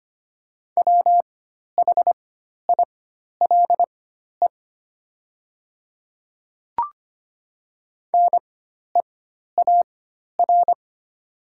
Morse code, heard as a keyed tone, about 25 wpm, spaced as for 12 wpm, 700 Hz. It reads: WHILE NEAR
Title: 4x Character Spacing - Top 500 Words - 25wpm